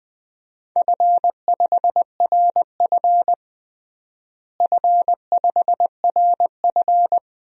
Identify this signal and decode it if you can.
Morse code, heard as a keyed tone, about 20 wpm, 700 Hz. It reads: F5RF F5RF
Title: DXCC-100 Call Signs - 4 Characters - Repeated 2x - 20wpm